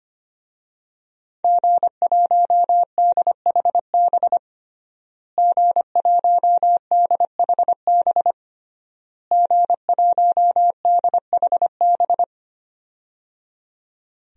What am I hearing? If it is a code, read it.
Morse code, heard as a keyed tone, about 25 wpm, 700 Hz. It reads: G1DHB G1DHB G1DHB
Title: DXCC-100 Call Signs - 5 Characters - Repeated 3x - 25wpm